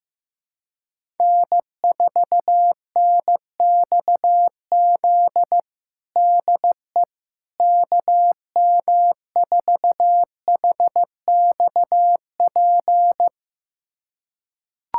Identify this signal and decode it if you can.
Morse code, heard as a keyed tone, about 15 wpm, 700 Hz. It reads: N4NXZ DE KM4HXP
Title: USA Callsign DE Callsign 15wpm